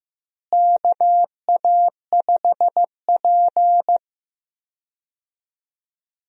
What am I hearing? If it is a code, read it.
Morse code, heard as a keyed tone, about 15 wpm, 700 Hz. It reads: KA5P
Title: USA Call Signs - 4 Characters - Repeated 2x - 15wpm